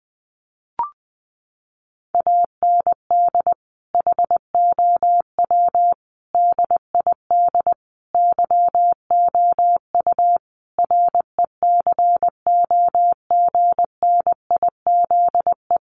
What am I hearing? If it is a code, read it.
Morse code, heard as a keyed tone, about 20 wpm, 700 Hz. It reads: AND HOW DID YOU RECOGNIZE
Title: Sentences from Top 1400 Words 20wpm